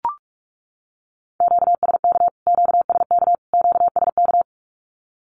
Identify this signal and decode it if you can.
Morse code, heard as a keyed tone, about 44 wpm, 700 Hz. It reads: QSK QSK QSK
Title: Q-Codes Repeated 3x 45wpm